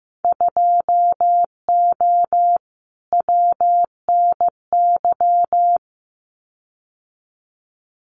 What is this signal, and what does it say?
Morse code, heard as a keyed tone, about 15 wpm, 700 Hz. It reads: 2O WNY
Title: Winter Field Day Exchanges Encore 15wpm